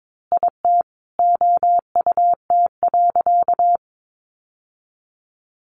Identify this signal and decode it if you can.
Morse code, heard as a keyed tone, about 22 wpm, 700 Hz. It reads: IT OUT.
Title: Sentences from Top 1400 Words 22wpm